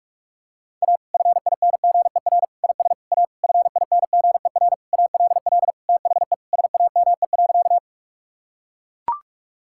Morse code, heard as a keyed tone, about 41 wpm, 700 Hz. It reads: A FINGER IS A FINGER ALL THE SAME.
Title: Sentences from Top 1600 Words 45wpm